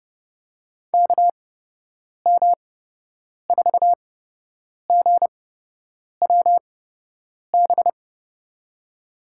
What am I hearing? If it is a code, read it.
Morse code, heard as a keyed tone, about 30 wpm, spaced as for 8 wpm, 700 Hz. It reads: KM4GWB
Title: USA Call Signs - 6 Characters - 8x Character Spacing - 30wpm